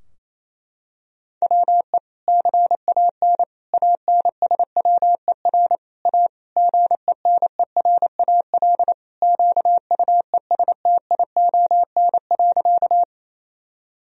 Morse code, heard as a keyed tone, about 28 wpm, 700 Hz. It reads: WE CAN ANSWER A GENERAL QUESTION.
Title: Sentences from Top 1000 Words - Review Part 3 - 28wpm